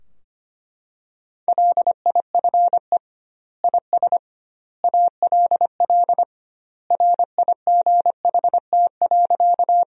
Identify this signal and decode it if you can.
Morse code, heard as a keyed tone, about 25 wpm, 700 Hz. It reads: LIFE IS ALL RIGHT.
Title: 2x Word-Spacing - Sentences from Top 300 Words - 25wpm